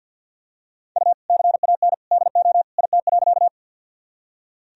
Morse code, heard as a keyed tone, about 47 wpm, 700 Hz. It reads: U CAN DO IT.
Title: Sentences from Top 100 Words and Abbreviations 50wpm